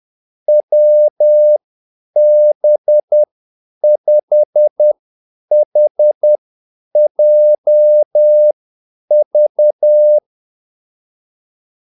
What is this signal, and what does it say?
Morse code, heard as a keyed tone, about 10 wpm, spaced as for 8 wpm, 600 Hz. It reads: WB5HJV
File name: Callsigns-10-8wpm